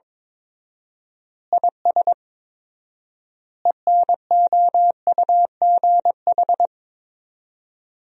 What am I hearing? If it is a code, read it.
Morse code, heard as a keyed tone, about 22 wpm, 700 Hz. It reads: IS ENOUGH
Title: 4x Word-Spacing - Sentences from Top 400 Words - 22wpm